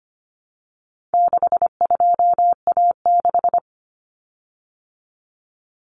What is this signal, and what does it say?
Morse code, heard as a keyed tone, about 25 wpm, 700 Hz. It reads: 62A6
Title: Common Cut Numbers - 4-Digit 25wpm